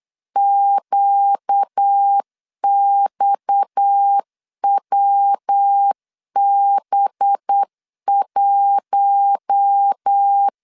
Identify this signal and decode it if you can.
Morse code, heard as a keyed tone, about 8 wpm, 800 Hz. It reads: QXWB1
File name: LEZIONI / 6 PKX0 / 6 PKX 8wpm 3041